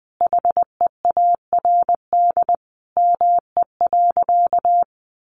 Morse code, heard as a keyed tone, about 20 wpm, 700 Hz. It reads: HEARD ME.
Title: Sentences from Top 400 Words 20wpm